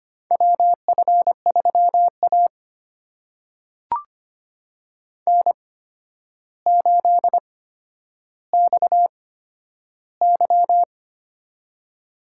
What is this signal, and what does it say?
Morse code, heard as a keyed tone, about 25 wpm, spaced as for 11 wpm, 700 Hz. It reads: WF3A N8XY
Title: USA Call Signs - 4 Characters - 8x Character Spacing - 25wpm